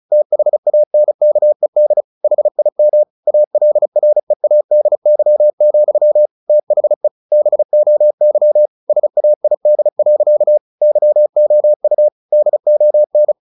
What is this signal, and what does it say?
Morse code, heard as a keyed tone, about 34 wpm, 600 Hz. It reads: THANKED HIM ALREADY, THE BOY SAID. YOU DON
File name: the old man and the sea / the old man and the sea 1 35wpm